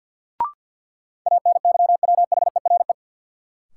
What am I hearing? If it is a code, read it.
Morse code, heard as a keyed tone, about 53 wpm, 700 Hz. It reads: ANYWHERE
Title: Sentences from Top 1400 Words - New Word Drill Alt - 50wpm